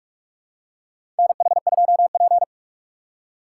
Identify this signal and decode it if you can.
Morse code, heard as a keyed tone, about 46 wpm, 700 Hz. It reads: NH2P
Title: USA Call Signs - 4 Characters - 4x Character Spacing - 45wpm